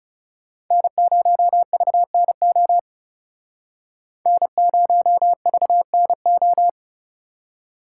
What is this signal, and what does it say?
Morse code, heard as a keyed tone, about 38 wpm, 700 Hz. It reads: N0VNO N0VNO